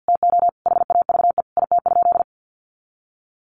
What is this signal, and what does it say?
Morse code, heard as a keyed tone, about 53 wpm, 700 Hz. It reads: TO HAVE IT?
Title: Sentences from Top 100 Words 50wpm